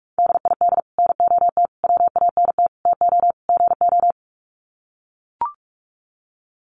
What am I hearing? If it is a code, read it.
Morse code, heard as a keyed tone, about 44 wpm, 700 Hz. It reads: DID NOT WANT TO GO
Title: Sets of 5 Words 45wpm